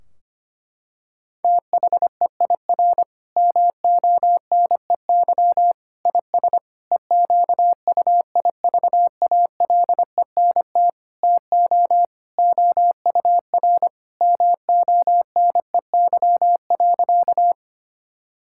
Morse code, heard as a keyed tone, about 25 wpm, 700 Hz. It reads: THEIR MONEY IS EQUIVALENT TO OUR MONEY.
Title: Sentences from Top 1600 Words 25wpm